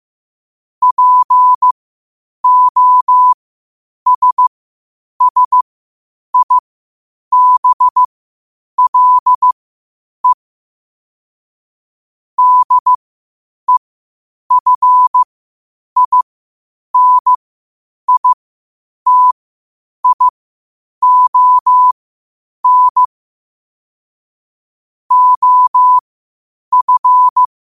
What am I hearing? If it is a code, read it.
Morse code, heard as a keyed tone, about 15 wpm, spaced as for 8 wpm, 1000 Hz. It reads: POSSIBLE DEFINITION OF
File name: qotdfiles / 8wpm